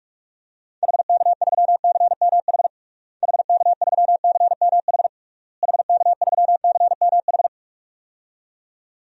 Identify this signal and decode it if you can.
Morse code, heard as a keyed tone, about 41 wpm, 700 Hz. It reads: HK3CMH HK3CMH HK3CMH
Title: DXCC-100 Call Signs - 6 Characters - Repeated 3x - 45wpm